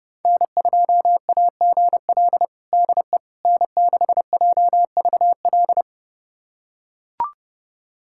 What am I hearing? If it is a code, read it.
Morse code, heard as a keyed tone, about 32 wpm, 700 Hz. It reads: N2AGL DE N6JVL
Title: USA Callsign DE Callsign - 5 Characters - 30wpm